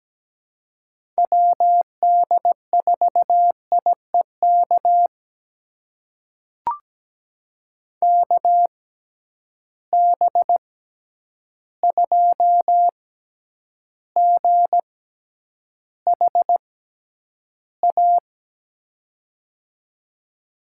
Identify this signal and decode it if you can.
Morse code, heard as a keyed tone, about 17 wpm, spaced as for 9 wpm, 700 Hz. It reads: WD4IEK KB2GHA
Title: USA Call Signs - 6 Characters - 6x Character Spacing - 17wpm